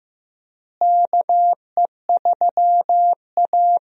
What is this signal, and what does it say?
Morse code, heard as a keyed tone, about 15 wpm, 700 Hz. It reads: KE3A
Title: USA Call Signs - 4 Characters - Repeated 2x - 15wpm